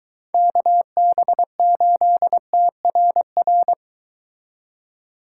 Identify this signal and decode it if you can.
Morse code, heard as a keyed tone, about 23 wpm, 700 Hz. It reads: KB8TRR